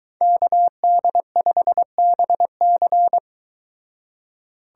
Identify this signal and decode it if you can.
Morse code, heard as a keyed tone, about 22 wpm, 700 Hz. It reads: KD5BC